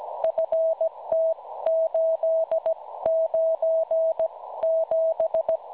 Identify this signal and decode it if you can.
Morse code, heard as a keyed tone, about 17 wpm, spaced as for 14 wpm, 650 Hz. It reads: FT897
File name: cwronde / websdr recording .6kHz